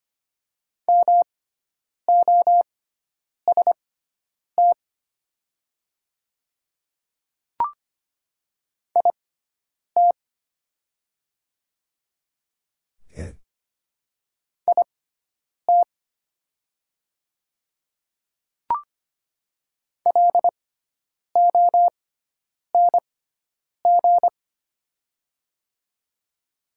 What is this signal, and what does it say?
Morse code, heard as a keyed tone, about 25 wpm, spaced as for 9 wpm, 700 Hz. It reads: MOST IT IT LONG
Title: 6x Character Spacing - Top 100 Words - 25wpm